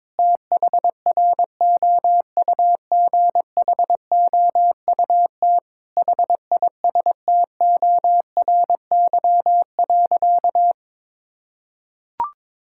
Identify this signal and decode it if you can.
Morse code, heard as a keyed tone, about 22 wpm, 700 Hz. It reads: THROUGHOUT HISTORY.
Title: Sentences from Top 1000 Words - Review Part 5 - 22wpm